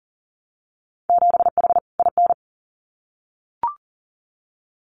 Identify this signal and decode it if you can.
Morse code, heard as a keyed tone, about 40 wpm, 700 Hz. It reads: 7H IN
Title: Winter Field Day Exchanges Encore 40wpm